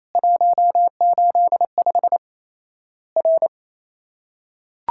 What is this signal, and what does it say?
Morse code, heard as a keyed tone, about 28 wpm, 700 Hz. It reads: 185 R
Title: Worked All Europe DX Contest - QTC Exchange - 28wpm